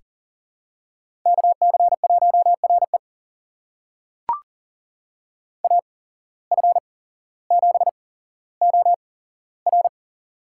Morse code, heard as a keyed tone, about 38 wpm, spaced as for 17 wpm, 700 Hz. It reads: KC1RE AF7OR
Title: USA Call Signs - 5 Characters - 8x Character Spacing - 40wpm